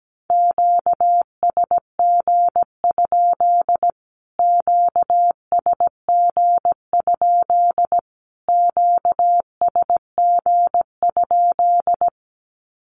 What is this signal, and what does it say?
Morse code, heard as a keyed tone, about 17 wpm, 700 Hz. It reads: QSG? QSG? QSG?
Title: Q-Codes Repeated 3x 17wpm